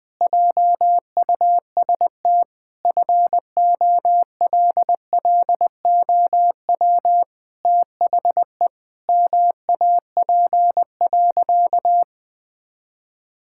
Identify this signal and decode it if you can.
Morse code, heard as a keyed tone, about 20 wpm, 700 Hz. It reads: JUST FOLLOW THE MAP.
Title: Sentences with 8-Word Phrases - 20wpm